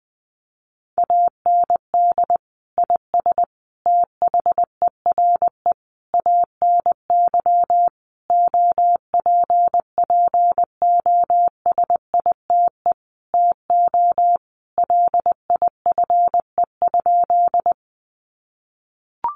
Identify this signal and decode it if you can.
Morse code, heard as a keyed tone, about 20 wpm, 700 Hz. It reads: AND IS THERE ANY OPPOSITE TO LIFE?
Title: Sentences from Top 1300 Words 20wpm